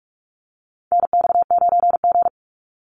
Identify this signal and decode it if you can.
Morse code, heard as a keyed tone, about 46 wpm, 700 Hz. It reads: NX9G
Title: Call Signs - 45wpm